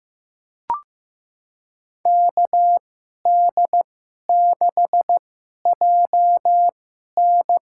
Morse code, heard as a keyed tone, about 16 wpm, spaced as for 11 wpm, 700 Hz. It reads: KD6JN
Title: USA Call Signs - 2x Character Spacing - 15wpm